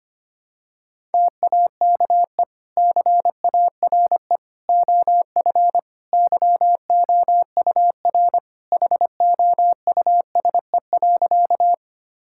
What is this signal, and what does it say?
Morse code, heard as a keyed tone, about 25 wpm, 700 Hz. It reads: TAKE CARE OF YOUR HOUSE.
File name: practice / 25wpm / Sentences with 2-Word Phrases - 25wpm